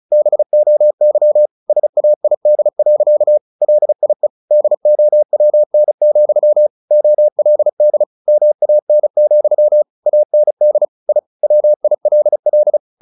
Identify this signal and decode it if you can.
Morse code, heard as a keyed tone, about 34 wpm, 600 Hz. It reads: BOY SAID. LIE DOWN, OLD MAN, AND I WILL
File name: the old man and the sea 2 35wpm